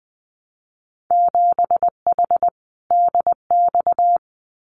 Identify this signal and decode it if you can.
Morse code, heard as a keyed tone, about 20 wpm, 700 Hz. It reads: 7H DX
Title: Winter Field Day Exchanges Encore 20wpm